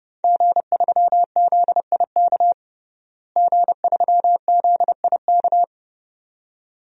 Morse code, heard as a keyed tone, about 30 wpm, 700 Hz. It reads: G3ZIK G3ZIK
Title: DXCC-100 Call Signs - 5 Characters - Repeated 2x - 30wpm